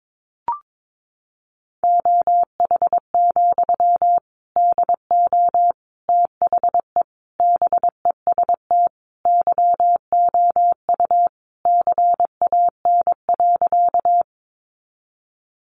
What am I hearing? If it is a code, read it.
Morse code, heard as a keyed tone, about 22 wpm, 700 Hz. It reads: OH, DO THE BEST YOU CAN.
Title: Sentences from Top 500 Words 22wpm